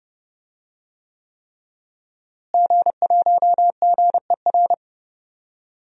Morse code, heard as a keyed tone, about 31 wpm, 700 Hz. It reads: G1GER
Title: DXCC-100 Call Signs - 5 Characters - Repeated 3x - 30wpm